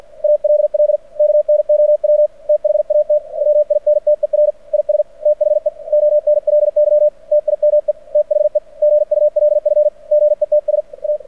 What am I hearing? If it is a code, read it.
Morse code, heard as a keyed tone, about 51 wpm, 600 Hz. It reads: ALL KNOW THAT WINTER IS THE ONLY TIME THE GULF GETS A
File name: archived / CRSnet